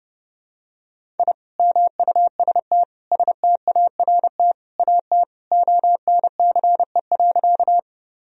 Morse code, heard as a keyed tone, about 30 wpm, 700 Hz. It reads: I MUST START AT ONCE.
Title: Sentences from Top 1000 Words - Review Part 1 - 30wpm